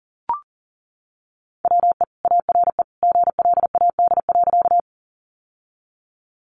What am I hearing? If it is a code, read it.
Morse code, heard as a keyed tone, about 40 wpm, 700 Hz. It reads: WE ARE GLAD.
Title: Sentences from Top 800 Words 40wpm